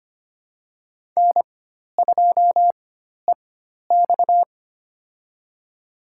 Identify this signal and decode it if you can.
Morse code, heard as a keyed tone, about 25 wpm, spaced as for 12 wpm, 700 Hz. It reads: N2EX
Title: USA Call Signs - 4x Character Spacing - 25wpm